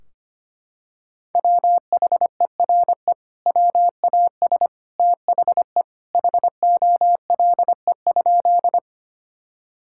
Morse code, heard as a keyed tone, about 25 wpm, 700 Hz. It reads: WHERE WAS THE HOLE?
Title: Sentences from Top 1600 Words 25wpm